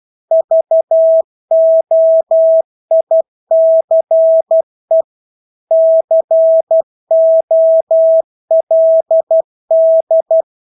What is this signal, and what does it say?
Morse code, heard as a keyed tone, about 12 wpm, 650 Hz. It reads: VOICE COLD
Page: Engelske ord 12 wpm